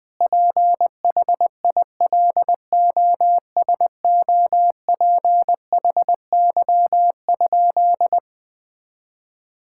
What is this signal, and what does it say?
Morse code, heard as a keyed tone, about 20 wpm, 700 Hz. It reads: PHILOSOPHY?
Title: Sentences from Top 1000 Words - Review Part 6 - 20wpm